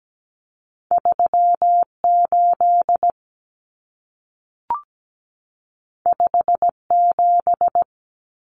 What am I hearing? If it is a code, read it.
Morse code, heard as a keyed tone, about 17 wpm, 700 Hz. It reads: 38 57
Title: Numbers - 2-Digit 17wpm